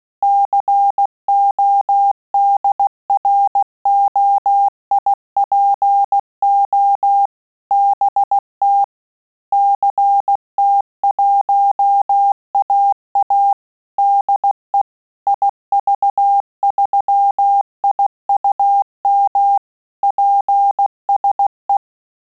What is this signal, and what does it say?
Morse code, heard as a keyed tone, about 16 wpm, 800 Hz. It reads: CODROIPO BT CT1AA DE IV3IUM PSE
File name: QSO CW 75 CPM